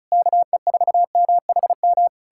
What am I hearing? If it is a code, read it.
Morse code, heard as a keyed tone, about 40 wpm, 700 Hz. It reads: KE4MHM